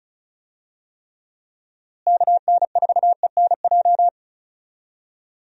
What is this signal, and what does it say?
Morse code, heard as a keyed tone, about 34 wpm, 700 Hz. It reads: KN4ENJ
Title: USA Call Signs - 6 Characters - Repeated 2x - 35wpm